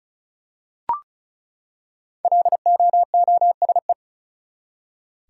Morse code, heard as a keyed tone, about 36 wpm, 700 Hz. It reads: LOOSE
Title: Five Letter Words - 35wpm